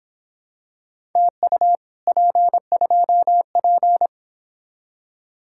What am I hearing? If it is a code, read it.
Morse code, heard as a keyed tone, about 26 wpm, 700 Hz. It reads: TU P2P
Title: Speed-Racing Common QSO Phrases - 22wpm